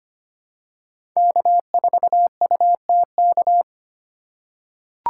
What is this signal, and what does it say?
Morse code, heard as a keyed tone, about 25 wpm, 700 Hz. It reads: K4UTK
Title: USA Call Signs - 5 Characters - 3x Character Spacing - 25wpm